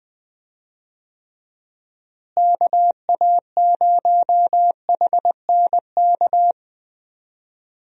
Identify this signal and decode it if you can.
Morse code, heard as a keyed tone, about 20 wpm, 700 Hz. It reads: KA0HNK
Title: USA Call Signs - 6 Characters - Repeated 3x - 20wpm